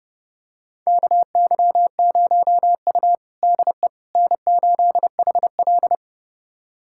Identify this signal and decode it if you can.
Morse code, heard as a keyed tone, about 30 wpm, 700 Hz. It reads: KY0U DE N8HL
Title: USA Callsign DE Callsign - 4 Characters - 30wpm